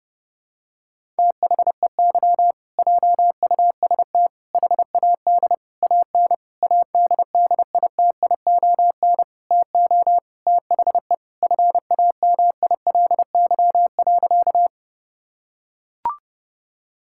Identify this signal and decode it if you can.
Morse code, heard as a keyed tone, about 30 wpm, 700 Hz. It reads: THEY JUST HAD AN ADDITION TO THE FAMILY.
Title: Sentences from Top 1000 Words - Review Part 6 - 30wpm